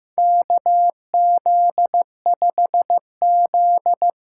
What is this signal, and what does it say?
Morse code, heard as a keyed tone, about 15 wpm, 700 Hz. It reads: KZ5Z
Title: K1USN SST QSO Practice - 15wpm